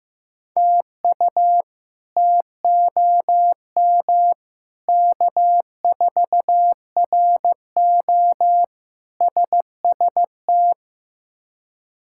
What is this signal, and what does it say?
Morse code, heard as a keyed tone, about 15 wpm, 700 Hz. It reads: TU TOM K4RO SST
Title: K1USN SST QSO Practice - 15wpm